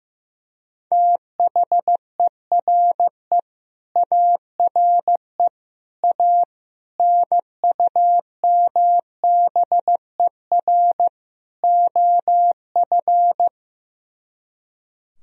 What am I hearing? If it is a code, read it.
Morse code, heard as a keyed tone, about 15 wpm, 700 Hz. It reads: THERE ARE A NUMBER OF
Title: Speed-Racing - Sets of 5 Words 15wpm